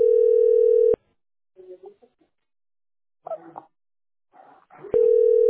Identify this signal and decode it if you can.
Ringback tone (North America), 440 + 480 Hz.